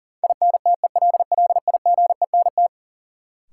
Morse code, heard as a keyed tone, about 40 wpm, 700 Hz. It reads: INTELLIGENT
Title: Sentences from Top 1000 Words - Review Part 7 - New Word Drill - 40wpm